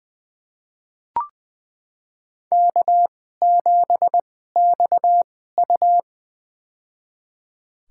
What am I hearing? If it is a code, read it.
Morse code, heard as a keyed tone, about 20 wpm, spaced as for 14 wpm, 700 Hz. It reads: K7XU
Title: USA Call Signs - 2x Character Spacing - 20wpm